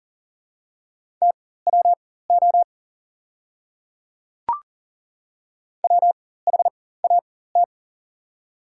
Morse code, heard as a keyed tone, about 40 wpm, spaced as for 19 wpm, 690 Hz. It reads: TWO WHAT
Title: 4x Character Spacing - Top 100 Words - 40wpm